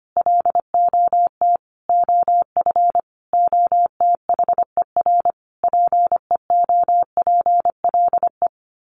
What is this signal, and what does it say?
Morse code, heard as a keyed tone, about 25 wpm, 700 Hz. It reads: LOT OF OTHER PEOPLE
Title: Sets of 5 Words Encore2 25wpm